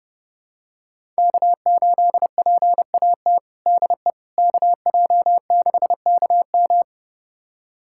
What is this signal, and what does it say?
Morse code, heard as a keyed tone, about 32 wpm, 700 Hz. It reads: K8PAT DE KJ6KM
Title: USA Callsign DE Callsign - 5 Characters - 30wpm